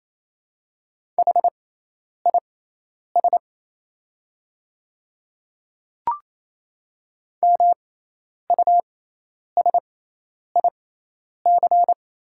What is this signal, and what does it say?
Morse code, heard as a keyed tone, about 28 wpm, spaced as for 10 wpm, 700 Hz. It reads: HIS MUSIC
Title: 6x Character Spacing - Top 300 Words - 28wpm